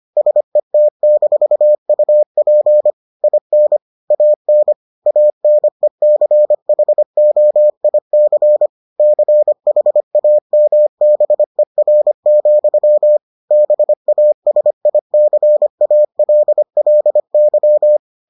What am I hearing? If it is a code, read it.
Morse code, heard as a keyed tone, about 25 wpm, 600 Hz. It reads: SET-UP IN AN ANECHOIC CHAMBER, BASICALLY